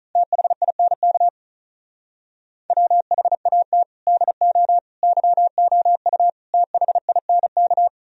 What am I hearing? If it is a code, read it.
Morse code, heard as a keyed tone, about 35 wpm, 700 Hz. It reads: THINK WHAT DO YOU THINK
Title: Speed-Racing - Sets of 4 Words 35wpm